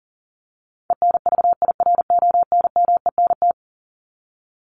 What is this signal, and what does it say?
Morse code, heard as a keyed tone, about 40 wpm, 700 Hz. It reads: ENVIRONMENT
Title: Top 501-750 Words 40wpm